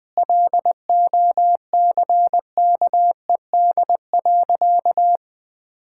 Morse code, heard as a keyed tone, about 20 wpm, 700 Hz. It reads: LOCKED.
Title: Sentences with 5-Word Phrases - 20wpm